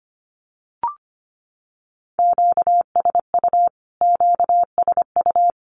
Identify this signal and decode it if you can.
Morse code, heard as a keyed tone, about 25 wpm, 700 Hz. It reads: QSU QSU
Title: Q-Codes Repeated 3x 25wpm